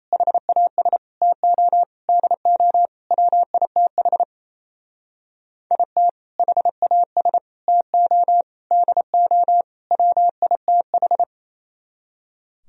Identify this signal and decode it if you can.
Morse code, heard as a keyed tone, about 35 wpm, 700 Hz. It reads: HAS TO DO WITH IT HAS TO DO WITH